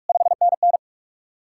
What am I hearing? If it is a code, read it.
Morse code, heard as a keyed tone, about 54 wpm, 700 Hz. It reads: 5NN
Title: Speed-Racing Common QSO Phrases Encore - 45wpm